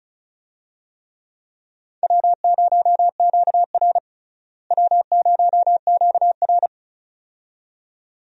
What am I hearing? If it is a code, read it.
Morse code, heard as a keyed tone, about 34 wpm, 700 Hz. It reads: W0QR W0QR
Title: USA Call Signs - 4 Characters - Repeated 2x - 35wpm